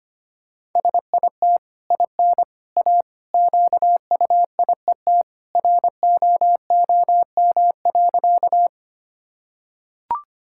Morse code, heard as a keyed tone, about 25 wpm, 690 Hz. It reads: SIT IN A QUIET ROOM.
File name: Easy Sentences with Hints - 25wpm